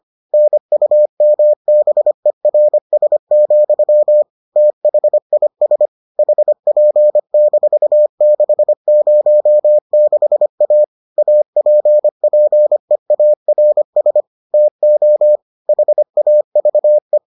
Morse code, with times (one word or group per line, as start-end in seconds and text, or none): 0.3-4.2 NUMBERS,
4.6-5.9 THIS
6.2-10.8 HP-606A
11.2-14.2 APPEARS
14.5-15.4 TO
15.7-17.2 HAVE